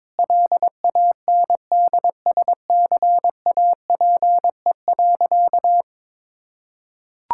Morse code, heard as a keyed tone, about 21 wpm, 700 Hz. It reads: LANDSCAPE.
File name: Difficult Sentences with Hints - 22wpm